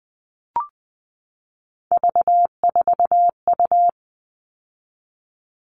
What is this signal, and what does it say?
Morse code, heard as a keyed tone, about 20 wpm, 700 Hz. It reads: V4U
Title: All Cut Numbers - 3-Digit 20wpm